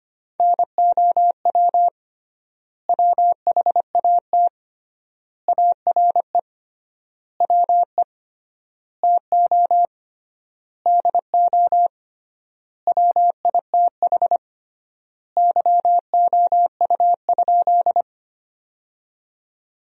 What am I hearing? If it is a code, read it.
Morse code, heard as a keyed tone, about 25 wpm, 700 Hz. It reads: NOW WHAT ARE WE TO DO WITH YOU?
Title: 3x Word-Spacing - Sentences from Top 100 Words - 25wpm